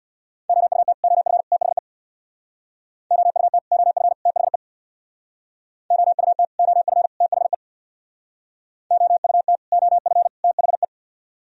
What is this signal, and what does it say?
Morse code, heard as a keyed tone, about 66 wpm, 700 Hz. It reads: OUT OF THE OUT OF THE OUT OF THE OUT OF THE